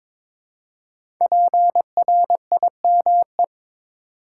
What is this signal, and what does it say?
Morse code, heard as a keyed tone, about 22 wpm, 700 Hz. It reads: PRIME
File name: 5 Letter Words - First 3 Letters Given - 22wpm